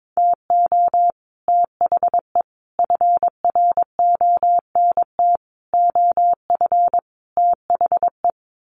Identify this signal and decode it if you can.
Morse code, heard as a keyed tone, about 22 wpm, 700 Hz. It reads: TO THE FRONT OF THE
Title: Sets of 5 Words 22wpm